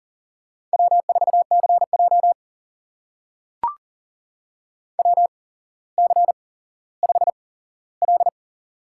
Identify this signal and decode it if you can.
Morse code, heard as a keyed tone, about 40 wpm, spaced as for 17 wpm, 700 Hz. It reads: W4CJ WC5L
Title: USA Call Signs - 4 Characters - 8x Character Spacing - 40wpm